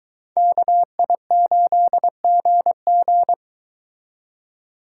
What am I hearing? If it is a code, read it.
Morse code, heard as a keyed tone, about 23 wpm, 700 Hz. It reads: KI8GG